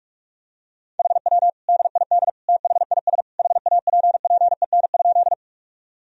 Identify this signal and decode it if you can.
Morse code, heard as a keyed tone, about 42 wpm, 700 Hz. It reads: HW DID THIS HAPPEN?
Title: Sentences from Top 400 Words and Abbreviations 45wpm